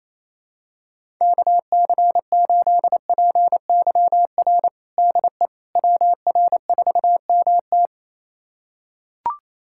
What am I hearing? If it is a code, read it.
Morse code, heard as a keyed tone, about 28 wpm, 700 Hz. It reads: KC8PYR DE WR4MT
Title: USA Callsign DE Callsign 28wpm